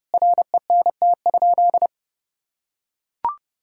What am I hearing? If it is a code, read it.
Morse code, heard as a keyed tone, about 30 wpm, 700 Hz. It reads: RENT?
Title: Sentences from Top 1000 Words - Review Part 5 - 30wpm